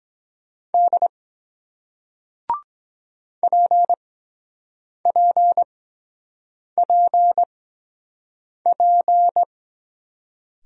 Morse code, heard as a keyed tone, about 22 wpm, spaced as for 7 wpm, 700 Hz. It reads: D PPPP